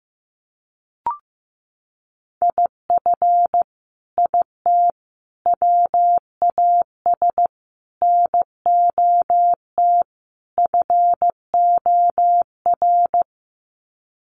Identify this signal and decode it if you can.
Morse code, heard as a keyed tone, about 15 wpm, 700 Hz. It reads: IF IT WAS NOT FOR
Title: Sets of 5 Words Encore 15wpm